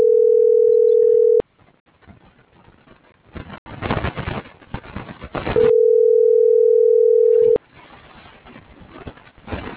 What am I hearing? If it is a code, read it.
Ringback tone (North America), 440 + 480 Hz.
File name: CENTRAL TERRESTRE / out claudiacruz - .65462